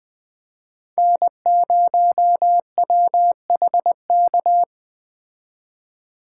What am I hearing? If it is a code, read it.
Morse code, heard as a keyed tone, about 21 wpm, 700 Hz. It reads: N0WHK